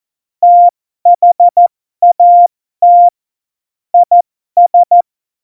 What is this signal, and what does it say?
Morse code, heard as a keyed tone, about 14 wpm, spaced as for 12 wpm, 700 Hz. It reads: THAT IS
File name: Morsekurs / Witze